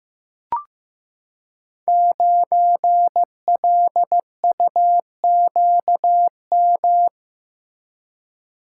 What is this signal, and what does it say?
Morse code, heard as a keyed tone, about 15 wpm, 700 Hz. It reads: 9LUQM
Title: Code Groups - 5 Letter:Numbers - 15wpm